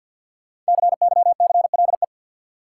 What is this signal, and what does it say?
Morse code, heard as a keyed tone, about 52 wpm, 700 Hz. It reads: CYCLE
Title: Sentences from Top 900 Words - New Word Drill - 50wpm